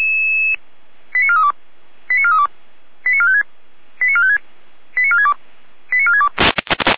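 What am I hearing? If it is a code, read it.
Five-tone selective call (ZVEI), heard as quick runs of five tones, tones of about 70 ms. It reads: 89432 89432 89456 89456 89462 89462
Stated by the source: Received on an SDR.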